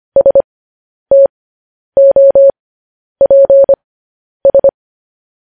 Morse code, heard as a keyed tone, about 25 wpm, spaced as for 10 wpm, 550 Hz. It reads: STOPS
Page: News Headlines in Morse Code at 10 WPM